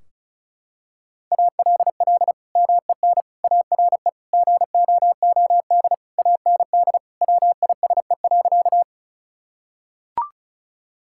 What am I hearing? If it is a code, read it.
Morse code, heard as a keyed tone, about 36 wpm, 700 Hz. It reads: ALL MEN ARE GOOD AND WISE.
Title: Sentences from Top 1600 Words 35wpm